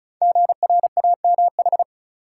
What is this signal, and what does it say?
Morse code, heard as a keyed tone, about 35 wpm, 690 Hz. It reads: GRAMH
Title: Code Groups - 5 Letters - 35wpm